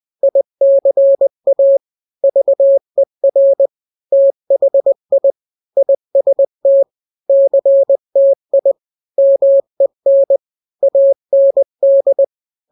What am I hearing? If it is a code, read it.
Morse code, heard as a keyed tone, about 20 wpm, 550 Hz. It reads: ICA VER THI IST CTI MEN AND
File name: n-grams / 3-grams 20 18 WPM